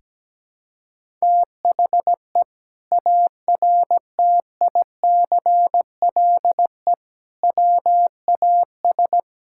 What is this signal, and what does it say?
Morse code, heard as a keyed tone, about 17 wpm, 700 Hz. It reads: THE ARTICLE WAS
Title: Sentences from Top 1000 Words - Review Part 5 - 17wpm